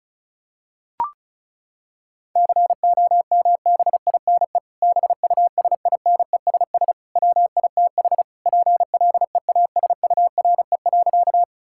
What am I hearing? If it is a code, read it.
Morse code, heard as a keyed tone, about 36 wpm, 700 Hz. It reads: COMBINE BUSINESS WITH PLEASURE.
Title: Sentences from Top 1600 Words 35wpm